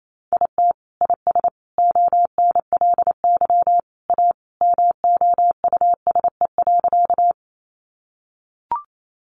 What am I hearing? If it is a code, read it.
Morse code, heard as a keyed tone, about 28 wpm, 700 Hz. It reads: IT IS ONLY A MOUSE.
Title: Sentences from Top 700 Words 28wpm